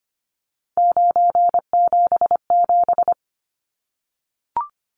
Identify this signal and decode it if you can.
Morse code, heard as a keyed tone, about 25 wpm, 700 Hz. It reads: 977
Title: Numbers - 3-Digit 25wpm